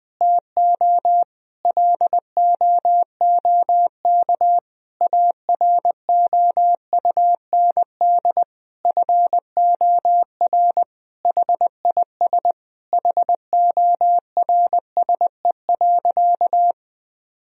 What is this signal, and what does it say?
Morse code, heard as a keyed tone, about 20 wpm, 700 Hz. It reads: TO LOOK AROUND FOR HIS HORSE.
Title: Sentences from Top 1000 Words - Review Part 3 - 20wpm